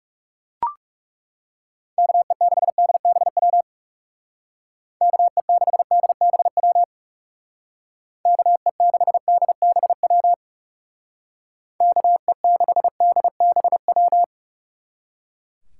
Morse code, heard as a keyed tone, about 37 wpm, 700 Hz. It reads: KE6DBW KE6DBW KE6DBW KE6DBW
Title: Speed Racing - USA 2x3 Call Signs 30wpm